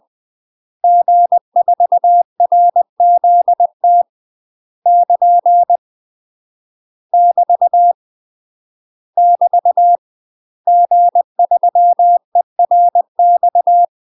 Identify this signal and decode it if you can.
Morse code, heard as a keyed tone, about 20 wpm, 700 Hz. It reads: G4RZT <KN> <BT> <BT> G3ERX
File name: UK MASTER QSOs TRAINING / 20WPM / UK MASTER QSOs 0007